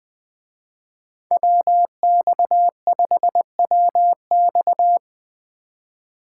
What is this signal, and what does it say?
Morse code, heard as a keyed tone, about 24 wpm, 700 Hz. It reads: WX5WX
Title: Speed Racing - USA Call Signs 20wpm